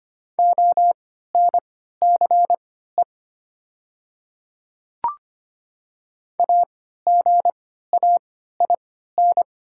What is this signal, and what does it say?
Morse code, heard as a keyed tone, about 25 wpm, spaced as for 14 wpm, 700 Hz. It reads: ONCE AGAIN
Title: 3x Character Spacing - Top 300 Words - 25wpm